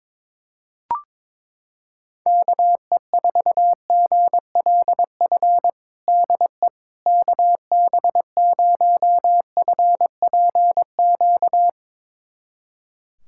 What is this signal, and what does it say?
Morse code, heard as a keyed tone, about 22 wpm, 700 Hz. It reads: KE4GLF DE KB0FPQ
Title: USA Callsign DE Callsign 22wpm